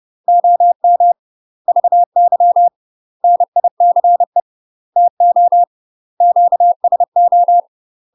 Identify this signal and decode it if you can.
Morse code, heard as a keyed tone, about 30 wpm, 700 Hz. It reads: OM VY NICE TO QSO
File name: UK MASTER QSOs TRAINING / 30WPM / UK MASTER QSOs 0009